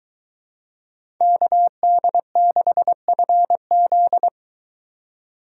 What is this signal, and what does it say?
Morse code, heard as a keyed tone, about 22 wpm, 700 Hz. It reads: KD6FZ